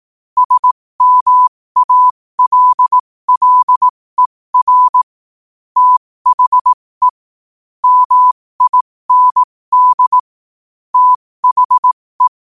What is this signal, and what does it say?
Morse code, heard as a keyed tone, about 18 wpm, spaced as for 15 wpm, 1000 Hz. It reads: SMALLER THE MIND THE
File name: qotdfiles / 15wpm